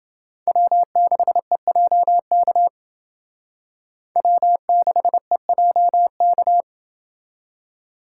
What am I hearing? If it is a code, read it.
Morse code, heard as a keyed tone, about 28 wpm, 700 Hz. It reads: W6EJK W6EJK